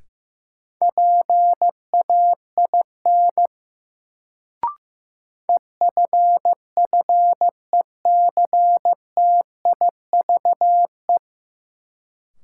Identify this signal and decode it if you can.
Morse code, heard as a keyed tone, about 15 wpm, 700 Hz. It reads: PAIN EFFECTIVE
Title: Sentences from Top 1600 Words - New Word Drill Alt - 15wpm